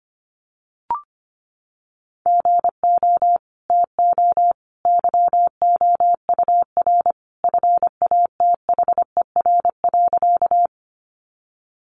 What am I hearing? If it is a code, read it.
Morse code, heard as a keyed tone, about 25 wpm, 700 Hz. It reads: GO TO YOUR FATHER.
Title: Sentences from Top 300 Words 25wpm